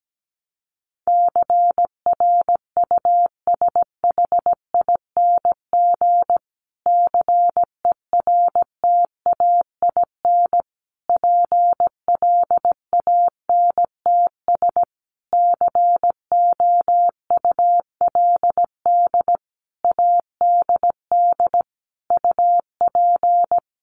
Morse code, heard as a keyed tone, about 17 wpm, 700 Hz. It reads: CRUSHING CERTAIN PLANTS COULD ADD UP
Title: A Sound Of Thunder 17wpm